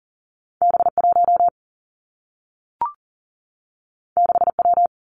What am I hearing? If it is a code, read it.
Morse code, heard as a keyed tone, about 39 wpm, 700 Hz. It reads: B 1 6 W
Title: Two Challenging Letter-Number Combinations 40wpm